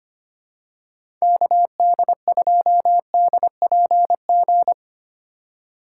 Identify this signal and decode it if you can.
Morse code, heard as a keyed tone, about 25 wpm, 700 Hz. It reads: KD2DPG
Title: USA Call Signs - 6 Characters - 3x Character Spacing - 25wpm